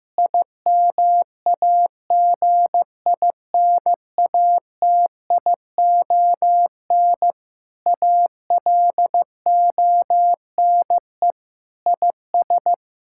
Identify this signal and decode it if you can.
Morse code, heard as a keyed tone, about 15 wpm, 700 Hz. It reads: IMAGINATION ALONE IS
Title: Sentences from Top 1600 Words 15wpm